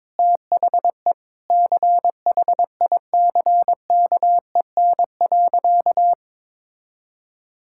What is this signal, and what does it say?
Morse code, heard as a keyed tone, about 22 wpm, 700 Hz. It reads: THE CHICKEN.
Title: Sentences from Top 1000 Words - Review Part 5 - 22wpm